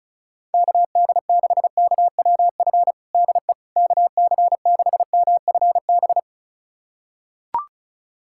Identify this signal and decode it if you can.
Morse code, heard as a keyed tone, about 34 wpm, 700 Hz. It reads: KD6KWF DE KC6MFB
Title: USA Callsign DE Callsign - 6 Characters - 35wpm